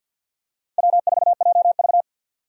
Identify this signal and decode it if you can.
Morse code, heard as a keyed tone, about 51 wpm, 700 Hz. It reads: W4JV
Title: USA Call Signs - 4 Characters - 6x Character Spacing - 50wpm